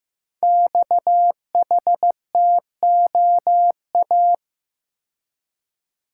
Morse code, heard as a keyed tone, about 15 wpm, 700 Hz. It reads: XHTOA
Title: Code Groups - 5 Letters - 15wpm